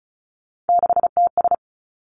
Morse code, heard as a keyed tone, about 36 wpm, 700 Hz. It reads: 6TS
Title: Three Challenging Letter-Number Combinations 35wpm